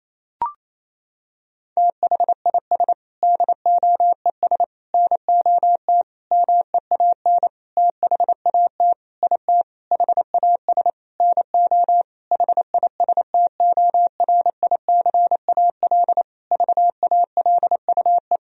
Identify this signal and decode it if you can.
Morse code, heard as a keyed tone, about 28 wpm, 700 Hz. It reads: THIS DOES NOT MEAN THAT IT HAS NO HISTORICAL VALUE
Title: Sentences from Top 1000 Words - Review Part 4 - 28wpm